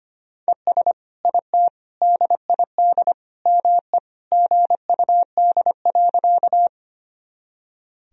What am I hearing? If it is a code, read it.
Morse code, heard as a keyed tone, about 25 wpm, 700 Hz. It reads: ES IT DID ME GUD.
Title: Sentences from Top 200 Words and Abbreviations 25wpm